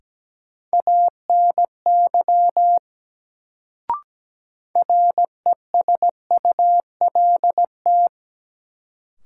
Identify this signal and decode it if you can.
Morse code, heard as a keyed tone, about 17 wpm, 700 Hz. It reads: ANY RESULT
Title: Sets of 3 Words - New Word Drill - 17wpm